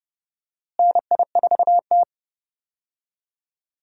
Morse code, heard as a keyed tone, about 30 wpm, 700 Hz. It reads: NI4T
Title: USA Call Signs - 4 Characters - Repeated 2x - 30wpm